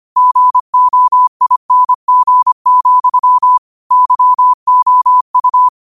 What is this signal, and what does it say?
Morse code, heard as a keyed tone, about 25 wpm, 1000 Hz. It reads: GOING, YOU